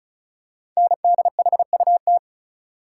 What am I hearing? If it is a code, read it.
Morse code, heard as a keyed tone, about 36 wpm, 700 Hz. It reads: NDHUT
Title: Code Groups - 5 Letters - 35wpm